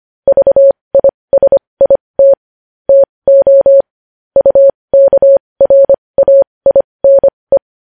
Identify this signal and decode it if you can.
Morse code, heard as a keyed tone, about 25 wpm, spaced as for 20 wpm, 550 Hz. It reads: VISIT TO UKRAINE
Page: News Headlines in Morse Code at 20 WPM